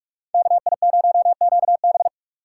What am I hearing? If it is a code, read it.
Morse code, heard as a keyed tone, about 46 wpm, 700 Hz. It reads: KI0QB
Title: USA Call Signs - 6x Character Spacing - 45wpm